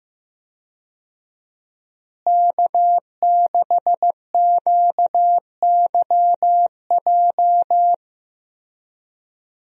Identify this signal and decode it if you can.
Morse code, heard as a keyed tone, about 15 wpm, 700 Hz. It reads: K6QYJ
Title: USA Call Signs - 5 Characters - Repeated 3x - 15wpm